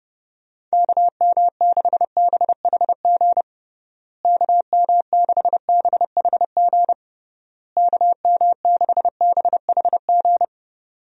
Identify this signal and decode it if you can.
Morse code, heard as a keyed tone, about 32 wpm, 700 Hz. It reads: KM6BHG KM6BHG KM6BHG
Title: USA Call Signs - 6 Characters - Repeated 3x - 30wpm